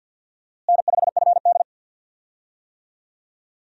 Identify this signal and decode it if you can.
Morse code, heard as a keyed tone, about 51 wpm, 700 Hz. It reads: N5FD
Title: USA Call Signs - 4 Characters - Repeated 2x - 50wpm